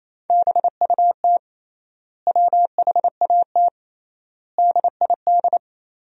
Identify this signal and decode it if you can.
Morse code, heard as a keyed tone, about 28 wpm, 700 Hz. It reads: BUT WHAT DID
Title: 3x Word-Spacing - Sentences from Top 100 Words - 28wpm